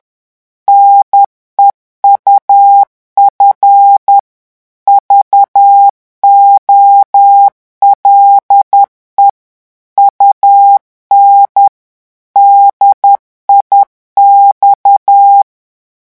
Morse code, heard as a keyed tone, about 10 wpm, 800 Hz. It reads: NEUF VOLE UN DIX
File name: morse mp3 / revision francais 10 mots minute